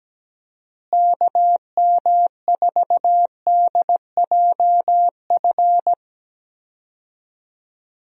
Morse code, heard as a keyed tone, about 17 wpm, 700 Hz. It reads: KM4DJF
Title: USA Call Signs - 6 Characters - Repeated 2x - 17wpm